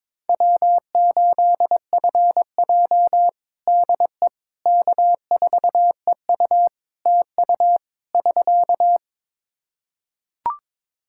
Morse code, heard as a keyed tone, about 22 wpm, 700 Hz. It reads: W8FJ DE K4EU TU <SK>
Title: Practice QSO - Exchange 4 - Ending - 22wpm